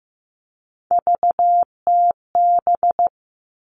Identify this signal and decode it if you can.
Morse code, heard as a keyed tone, about 15 wpm, 700 Hz. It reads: VTB
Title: All Cut Numbers - 3-Digit 15wpm